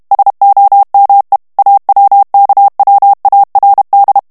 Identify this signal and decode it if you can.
Morse code, heard as a keyed tone, about 32 wpm, 800 Hz. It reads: SOME AWKWARD
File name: ANGLAIS texte clair 2 vitesse 30 mots minute